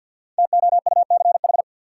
Morse code, heard as a keyed tone, about 52 wpm, 700 Hz. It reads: TOUCH
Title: Sentences from Top 1200 Words - New Word Drill Alt - 50wpm